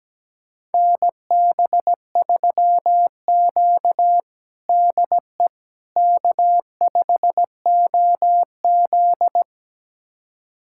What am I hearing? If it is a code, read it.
Morse code, heard as a keyed tone, about 17 wpm, 700 Hz. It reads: NB3Q DE K5OZ
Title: USA Callsign DE Callsign - 4 Characters - 17wpm